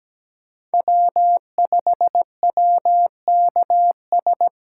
Morse code, heard as a keyed tone, about 20 wpm, 700 Hz. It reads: W5WKS
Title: Speed Racing - USA 1x3 Call Signs 17wpm